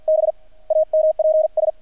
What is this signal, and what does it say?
Morse code, heard as a keyed tone, about 48 wpm, 630 Hz. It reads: B AMPS
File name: archived / CRSnet